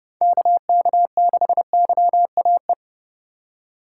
Morse code, heard as a keyed tone, about 35 wpm, 700 Hz. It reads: KK6YAE